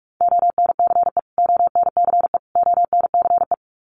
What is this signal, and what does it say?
Morse code, heard as a keyed tone, about 44 wpm, 700 Hz. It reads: ONCE ONCE ONCE
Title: Top 300 Words Repeated 45wpm